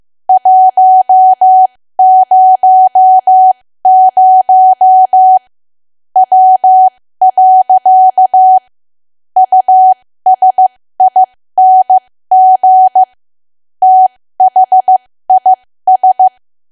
Morse code, heard as a keyed tone, about 15 wpm, spaced as for 13 wpm, 750 Hz. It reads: 100 W. USING THIS
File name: cwa / W1AW-Level2 / 13WPM